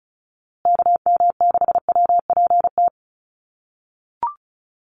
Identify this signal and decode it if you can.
Morse code, heard as a keyed tone, about 34 wpm, 700 Hz. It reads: KM6WPT
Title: USA Call Signs Repeated 3x 35wpm